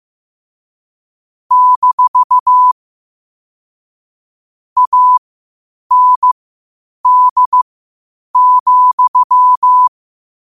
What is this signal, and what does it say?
Morse code, heard as a keyed tone, about 15 wpm, spaced as for 8 wpm, 1000 Hz. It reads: - AND,
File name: qotdfiles / 8wpm